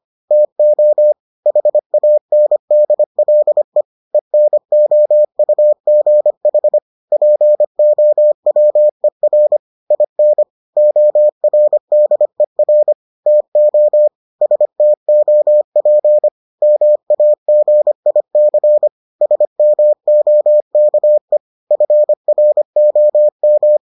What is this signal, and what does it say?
Morse code, heard as a keyed tone, about 25 wpm, 600 Hz. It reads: TO HANDLE ENOUGH POWER IN ORDER TO STOP MAGIC SMOKE FROM